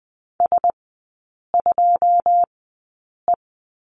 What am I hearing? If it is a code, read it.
Morse code, heard as a keyed tone, about 19 wpm, spaced as for 8 wpm, 700 Hz. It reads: S2E
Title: 081 Lesson 14e - Number 2 - Three Characters 20wpm